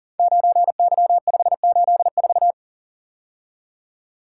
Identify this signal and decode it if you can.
Morse code, heard as a keyed tone, about 38 wpm, 700 Hz. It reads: 9Y584
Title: Code Groups - 5 Letter:Numbers - 40wpm